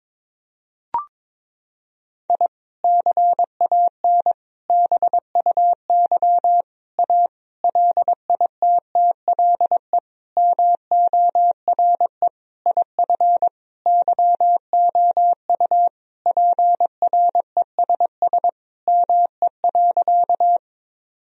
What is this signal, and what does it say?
Morse code, heard as a keyed tone, about 22 wpm, 700 Hz. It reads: I CAN BUY A LITTLE MORE IF YOU PRESS ME.
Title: Sentences from Top 1000 Words - Review Part 1 - 22wpm